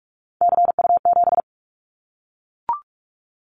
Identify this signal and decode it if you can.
Morse code, heard as a keyed tone, about 44 wpm, 700 Hz. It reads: CU7
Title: Callsign Prefixes for Top 100 DXCC - 45wpm